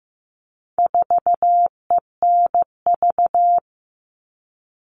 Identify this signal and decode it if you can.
Morse code, heard as a keyed tone, about 15 wpm, 700 Hz. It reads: 4ENV
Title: All Cut Numbers - 4-Digit 15wpm